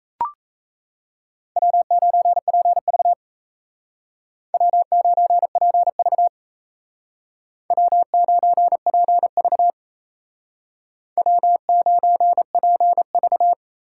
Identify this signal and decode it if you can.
Morse code, heard as a keyed tone, about 35 wpm, 700 Hz. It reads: W9PV W9PV W9PV W9PV